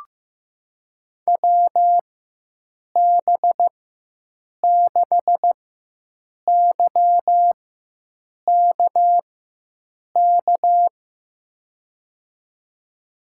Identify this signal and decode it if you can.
Morse code, heard as a keyed tone, about 15 wpm, spaced as for 7 wpm, 700 Hz. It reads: WB6YKK
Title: USA Call Signs - 4x Character Spacing - 15wpm